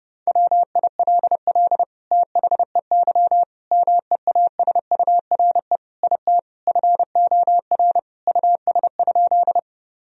Morse code, heard as a keyed tone, about 32 wpm, 700 Hz. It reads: WILL THEY MEASURE IT FOR US?
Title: Sentences from Top 1000 Words - Review Part 2 - 30wpm